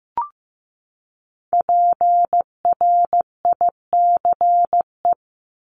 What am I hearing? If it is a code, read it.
Morse code, heard as a keyed tone, about 15 wpm, 700 Hz. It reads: PRICE
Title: Top 501-750 Words 15wpm